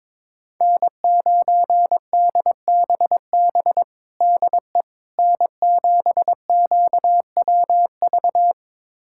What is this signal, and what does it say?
Morse code, heard as a keyed tone, about 21 wpm, 700 Hz. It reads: N9DBB DE N7QWV
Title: USA Callsign DE Callsign - 5 Characters - 22wpm